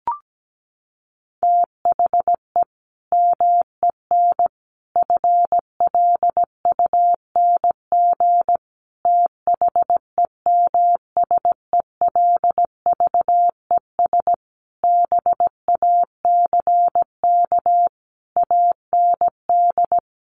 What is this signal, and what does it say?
Morse code, heard as a keyed tone, about 17 wpm, 700 Hz. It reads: THE MEN FLUNG THEMSELVES BACK AND
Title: A Sound Of Thunder 17wpm